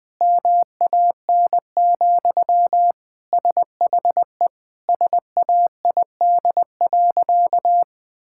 Morse code, heard as a keyed tone, about 20 wpm, 700 Hz. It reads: MAN, SHE SAID.
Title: Sentences from Top 1000 Words - Review Part 6 - 20wpm